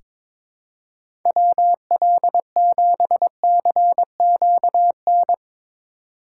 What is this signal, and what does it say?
Morse code, heard as a keyed tone, about 22 wpm, 700 Hz. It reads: WL7CQN
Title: USA Call Signs - 6 Characters - 6x Character Spacing - 22wpm